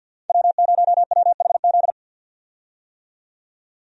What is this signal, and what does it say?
Morse code, heard as a keyed tone, about 49 wpm, 700 Hz. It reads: W0WHZ
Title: USA Call Signs - 5 Characters - 50wpm